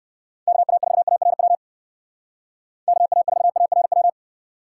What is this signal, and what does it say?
Morse code, heard as a keyed tone, about 54 wpm, 700 Hz. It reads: KA4ARW KA4ARW
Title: Speed Racing - USA 2x3 Call Signs 45wpm